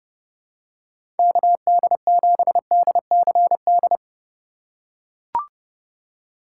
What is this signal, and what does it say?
Morse code, heard as a keyed tone, about 30 wpm, 700 Hz. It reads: KD7DCD
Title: USA Call Signs - 3x Character Spacing - 30wpm